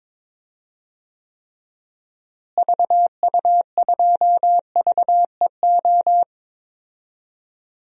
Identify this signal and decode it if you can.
Morse code, heard as a keyed tone, about 22 wpm, 700 Hz. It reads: VU2VEO
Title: DXCC-100 Call Signs - 6 Characters - Repeated 2x - 22wpm